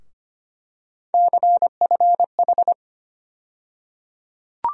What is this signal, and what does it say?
Morse code, heard as a keyed tone, about 25 wpm, 700 Hz. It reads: CFH
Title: USA Call Sign Postfixes - 3 Letters 25wpm